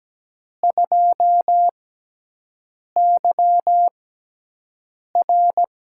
Morse code, heard as a keyed tone, about 17 wpm, spaced as for 6 wpm, 700 Hz. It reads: 2YR
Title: USA Call Signs - 6 Characters - 6x Character Spacing - 17wpm